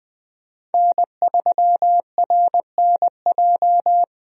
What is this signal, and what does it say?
Morse code, heard as a keyed tone, about 20 wpm, 700 Hz. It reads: N3RNJ
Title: USA Call Signs - 5 Characters - 20wpm